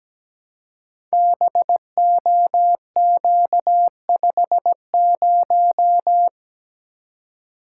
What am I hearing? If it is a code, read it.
Morse code, heard as a keyed tone, about 17 wpm, 700 Hz. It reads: BOQ50
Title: Code Groups - 5 Letter:Numbers - 17wpm